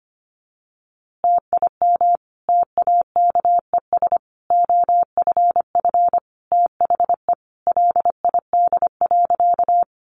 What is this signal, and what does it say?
Morse code, heard as a keyed tone, about 25 wpm, 700 Hz. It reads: TIM TAKES OFF THE LID.
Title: Easy Phrases 25wpm